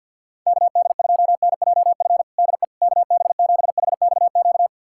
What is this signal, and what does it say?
Morse code, heard as a keyed tone, about 52 wpm, 700 Hz. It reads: KD2NJF DE KB7HXQ
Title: USA Callsign DE Callsign - 6 Characters - 50wpm